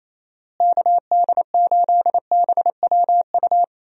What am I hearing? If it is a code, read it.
Morse code, heard as a keyed tone, about 27 wpm, 700 Hz. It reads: KD8BWU
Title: USA Call Signs - 6 Characters - Repeated 2x - 28wpm